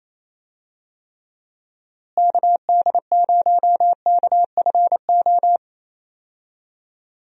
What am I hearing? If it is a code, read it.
Morse code, heard as a keyed tone, about 28 wpm, 700 Hz. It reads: KD0KFO
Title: USA Call Signs - 6 Characters - Repeated 3x - 28wpm